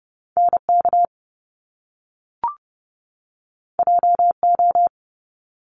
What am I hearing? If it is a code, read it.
Morse code, heard as a keyed tone, about 30 wpm, 700 Hz. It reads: NK JO
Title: Top Two Letter Combinations in Words 30wpm